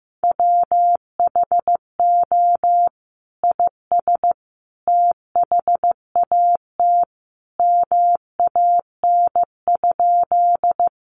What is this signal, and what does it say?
Morse code, heard as a keyed tone, about 15 wpm, 700 Hz. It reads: WHO IS THAT MAN?
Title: Sentences from Top 200 Words 15wpm